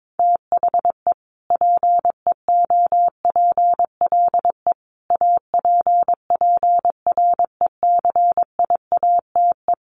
Morse code, heard as a keyed tone, about 22 wpm, 700 Hz. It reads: THE PEOPLE APPRECIATE
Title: Sentences from Top 1300 Words 22wpm